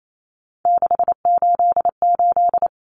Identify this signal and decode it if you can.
Morse code, heard as a keyed tone, about 28 wpm, 700 Hz. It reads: 688
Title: Numbers - 3-Digit 28wpm